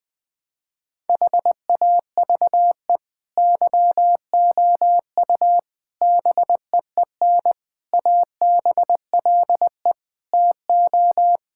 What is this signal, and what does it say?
Morse code, heard as a keyed tone, about 21 wpm, 700 Hz. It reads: HAVE YOU BEEN ABLE TO
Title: Sentences from Top 1000 Words - Review Part 6 - 20wpm